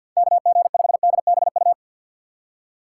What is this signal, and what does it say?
Morse code, heard as a keyed tone, about 50 wpm, 700 Hz. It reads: KG5DBU
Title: USA Call Signs - 6 Characters - 6x Character Spacing - 50wpm